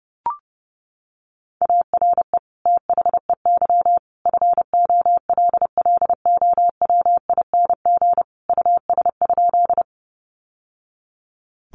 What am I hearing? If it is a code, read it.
Morse code, heard as a keyed tone, about 30 wpm, 700 Hz. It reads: ARE THEY FOLLOWING US?
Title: Sentences from Top 700 Words 30wpm